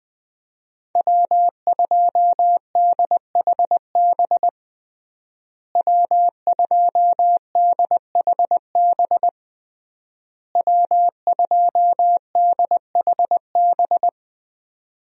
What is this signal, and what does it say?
Morse code, heard as a keyed tone, about 20 wpm, 700 Hz. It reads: W2DHB W2DHB W2DHB
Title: USA Call Signs - 5 Characters - Repeated 3x - 20wpm